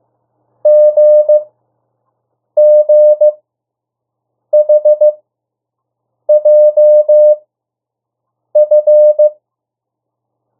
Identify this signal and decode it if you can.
Morse code, heard as a keyed tone, about 15 wpm, spaced as for 6 wpm, 600 Hz. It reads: GGHJF
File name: urok 1